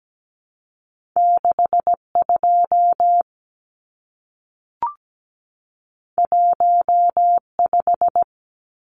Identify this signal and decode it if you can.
Morse code, heard as a keyed tone, about 17 wpm, 700 Hz. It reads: 62 15
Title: Numbers - 2-Digit 17wpm